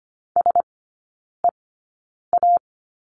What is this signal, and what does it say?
Morse code, heard as a keyed tone, about 25 wpm, spaced as for 9 wpm, 700 Hz. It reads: SEA
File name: Lesson 14e - Number 2 - Three Characters 25wpm